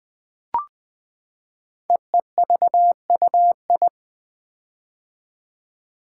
Morse code, heard as a keyed tone, about 20 wpm, 700 Hz. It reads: EEVUI
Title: Code Groups - 5 Letters - 20wpm